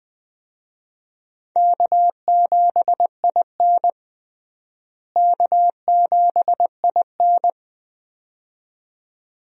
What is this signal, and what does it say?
Morse code, heard as a keyed tone, about 20 wpm, 700 Hz. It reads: K7IN K7IN
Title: USA Call Signs - 4 Characters - Repeated 2x - 20wpm